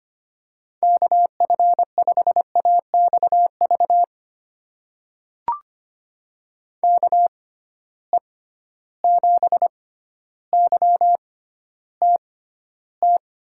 Morse code, heard as a keyed tone, about 25 wpm, spaced as for 13 wpm, 700 Hz. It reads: KF5AXV KE7YTT
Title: USA Call Signs - 6 Characters - 6x Character Spacing - 25wpm